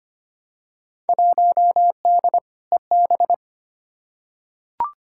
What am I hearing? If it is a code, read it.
Morse code, heard as a keyed tone, about 25 wpm, 700 Hz. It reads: 1D EB
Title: Field Day Exchanges - 25wpm